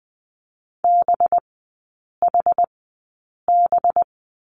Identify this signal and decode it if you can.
Morse code, heard as a keyed tone, about 20 wpm, spaced as for 8 wpm, 700 Hz. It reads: BHB
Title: 184 Lesson 30e - Letter b - Three Characters 20wpm